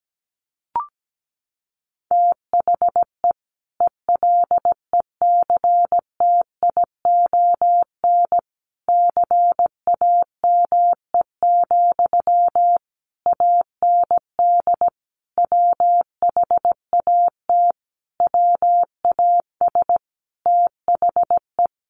Morse code, heard as a keyed tone, about 17 wpm, 700 Hz. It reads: THE ELECTION CAME, AND WHAT WAS THE
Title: Sentences from Top 1500 Words 17wpm